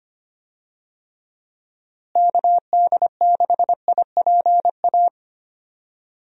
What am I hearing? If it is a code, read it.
Morse code, heard as a keyed tone, about 25 wpm, 700 Hz. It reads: KD6IPA
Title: USA Call Signs - 6 Characters - Repeated 2x - 25wpm